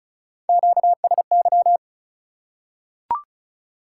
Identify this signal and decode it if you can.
Morse code, heard as a keyed tone, about 36 wpm, 700 Hz. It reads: QSY
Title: Top 180 QSO Elements 35wpm